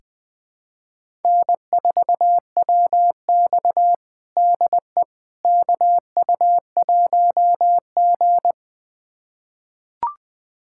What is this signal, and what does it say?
Morse code, heard as a keyed tone, about 21 wpm, 700 Hz. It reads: N4WX DE KU1G
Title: USA Callsign DE Callsign - 4 Characters - 20wpm